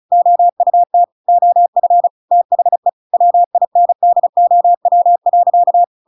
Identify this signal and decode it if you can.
Morse code, heard as a keyed tone, about 34 wpm, 700 Hz. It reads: OUT OF THE WINDOW.
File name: Pinocchio / Pinocchio ch5 35 35 ik1hge